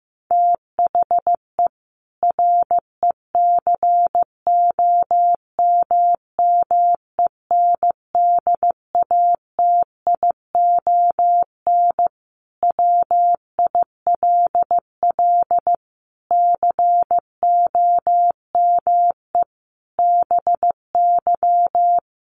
Morse code, heard as a keyed tone, about 14 wpm, 700 Hz. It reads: THE RECOMMENDATION WILL COME BY
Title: Sentences from Top 1400 Words 15wpm